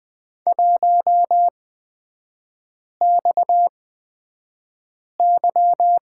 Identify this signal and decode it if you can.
Morse code, heard as a keyed tone, about 20 wpm, spaced as for 5 wpm, 700 Hz. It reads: 1XY
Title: 236 Lesson 38e - Letter q - Three Characters 20wpm